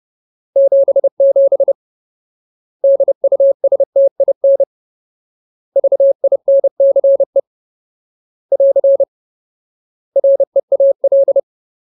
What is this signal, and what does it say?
Morse code, heard as a keyed tone, about 30 wpm, 550 Hz. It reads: 77 DUSTIN VINCE <AR> REAL
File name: Ultimate Word List / 30WPM Ultimate